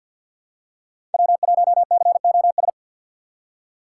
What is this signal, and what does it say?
Morse code, heard as a keyed tone, about 47 wpm, 700 Hz. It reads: W1COS
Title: USA Call Signs - 5 Characters - 6x Character Spacing - 50wpm